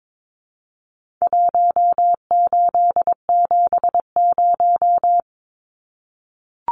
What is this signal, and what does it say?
Morse code, heard as a keyed tone, about 22 wpm, 700 Hz. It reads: 1870
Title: Numbers - Contest Style - 4-Digit 22wpm